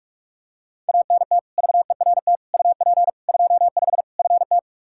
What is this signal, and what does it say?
Morse code, heard as a keyed tone, about 45 wpm, 700 Hz. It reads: ANT VERT UP 25 FT
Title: K1USN SST QSO Practice 45wpm